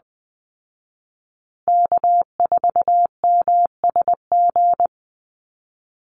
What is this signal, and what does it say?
Morse code, heard as a keyed tone, about 20 wpm, 700 Hz. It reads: K4MSG
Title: Call Signs Repeated 2x - 20wpm